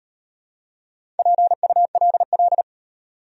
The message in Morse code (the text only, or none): PULL